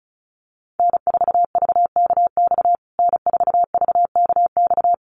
Morse code, heard as a keyed tone, about 34 wpm, 700 Hz. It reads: N4VKX N4VKX
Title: USA Call Signs Repeated 3x 35wpm